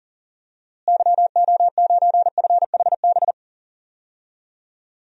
Morse code, heard as a keyed tone, about 40 wpm, 700 Hz. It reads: YO9FHB
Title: DXCC-100 Call Signs - 6 Characters - Repeated 2x - 40wpm